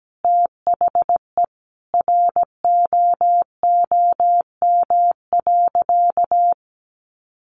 Morse code, heard as a keyed tone, about 17 wpm, 700 Hz. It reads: THE ROOM.
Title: Sentences from Top 400 Words 17wpm